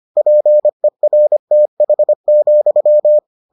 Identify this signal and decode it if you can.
Morse code, heard as a keyed tone, about 25 wpm, 600 Hz. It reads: PERTH,